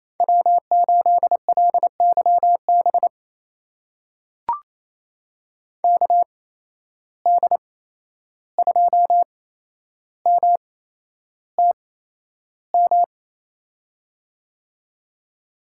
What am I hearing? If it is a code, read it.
Morse code, heard as a keyed tone, about 28 wpm, spaced as for 11 wpm, 700 Hz. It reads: W8LYB KD2MTM
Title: USA Call Signs - 8x Character Spacing - 28wpm